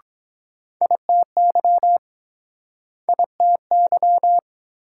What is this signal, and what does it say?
Morse code, heard as a keyed tone, about 23 wpm, 700 Hz. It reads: ITY ITY